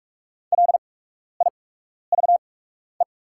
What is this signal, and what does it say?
Morse code, heard as a keyed tone, about 44 wpm, spaced as for 12 wpm, 690 Hz. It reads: LIVE
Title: 8x Character Spacing - Top 200 Words - 45wpm